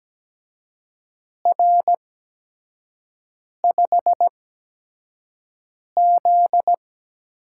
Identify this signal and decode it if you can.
Morse code, heard as a keyed tone, about 17 wpm, spaced as for 5 wpm, 700 Hz. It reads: R5Z
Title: USA Call Signs - 4 Characters - 8x Character Spacing - 17wpm